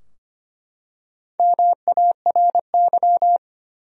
Morse code, heard as a keyed tone, about 25 wpm, 700 Hz. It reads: MARY
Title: USA Common Names - Max 4 Letters - 25wpm